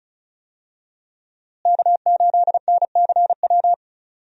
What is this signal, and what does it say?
Morse code, heard as a keyed tone, about 36 wpm, 700 Hz. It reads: K8NCW
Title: USA Call Signs - 5 Characters - Repeated 2x - 35wpm